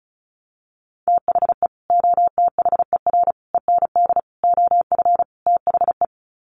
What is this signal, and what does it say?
Morse code, heard as a keyed tone, about 34 wpm, 700 Hz. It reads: THE OTHER END OF THE
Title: Sets of 5 Words 35wpm